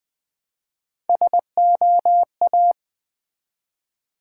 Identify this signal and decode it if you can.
Morse code, heard as a keyed tone, about 20 wpm, 700 Hz. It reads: SOA
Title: Build A Word - 1 Letter At A Time - 3 to 5-Letter Words - 20wpm